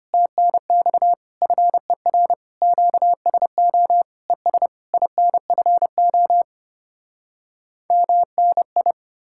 Morse code, heard as a keyed tone, about 33 wpm, 700 Hz. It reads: TNX FER QSO ES INFO MNI